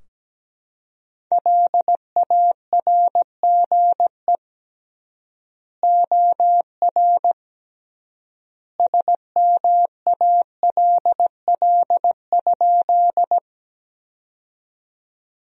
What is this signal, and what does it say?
Morse code, heard as a keyed tone, about 17 wpm, 700 Hz. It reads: LARGE OR SMALL?
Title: 3x Word-Spacing - Sentences from Top 200 Words - 17wpm